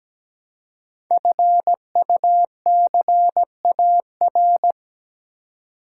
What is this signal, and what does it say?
Morse code, heard as a keyed tone, about 17 wpm, 700 Hz. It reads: FUCAR
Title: Code Groups - 5 Letter:Numbers - 17wpm